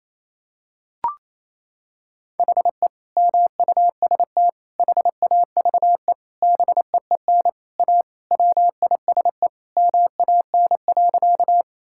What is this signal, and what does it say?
Morse code, heard as a keyed tone, about 28 wpm, 700 Hz. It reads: HE MUST HAVE BEEN A WISE MAN.
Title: Sentences from Top 1600 Words 28wpm